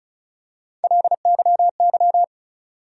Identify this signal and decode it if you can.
Morse code, heard as a keyed tone, about 36 wpm, 700 Hz. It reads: LYY
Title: USA Call Sign Postfixes - 3 Letters 35wpm